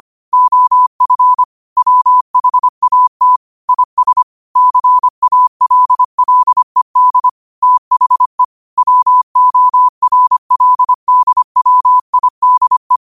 Morse code, heard as a keyed tone, about 25 wpm, 1000 Hz. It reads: OF WHAT IS CALLED THE WORLDWIDE
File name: qotdfiles / 25wpm